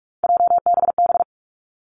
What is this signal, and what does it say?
Morse code, heard as a keyed tone, about 46 wpm, 700 Hz. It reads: JBB
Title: Three Challenging Letter-Number Combinations 45wpm